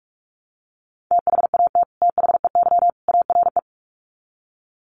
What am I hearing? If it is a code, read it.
Morse code, heard as a keyed tone, about 45 wpm, 700 Hz. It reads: THAT THEY ARE
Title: Sets of 3 Words 45wpm